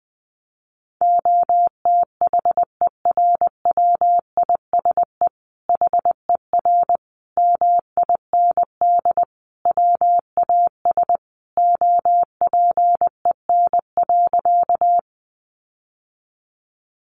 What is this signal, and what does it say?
Morse code, heard as a keyed tone, about 20 wpm, 700 Hz. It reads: OTHERWISE HER MIND WAS OPEN.
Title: Sentences from Top 1200 Words 20wpm